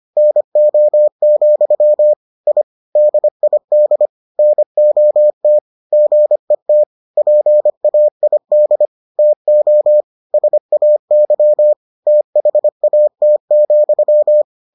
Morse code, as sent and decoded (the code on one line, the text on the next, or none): -. --- --..-- / .. / -.. .. -.. / -. --- - / --. - / .--. .- .. -.. / - --- / ... .- -.-- / - .... .- - --..--
NO, I DID NOT GET PAID TO SAY THAT,